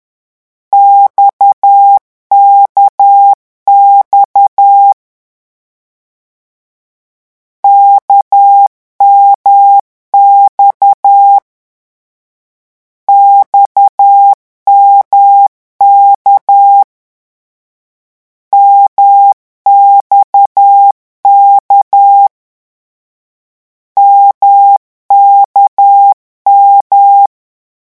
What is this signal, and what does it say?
Morse code, heard as a keyed tone, about 10 wpm, 800 Hz. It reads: XKX KMX XMK MXK MKM
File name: morse mp3 / lecon1-vitesse 10